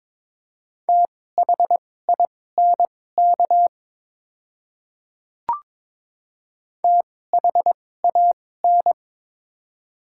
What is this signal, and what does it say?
Morse code, heard as a keyed tone, about 22 wpm, spaced as for 16 wpm, 700 Hz. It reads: THINK THAN
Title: 2x Character Spacing - Top 200 Words - 22wpm